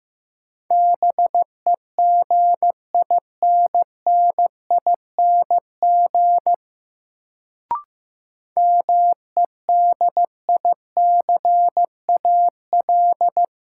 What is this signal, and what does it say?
Morse code, heard as a keyed tone, about 15 wpm, 700 Hz. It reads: BEGINNING MEDICAL
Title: Sentences from Top 800 Words - New Word Drill - 15wpm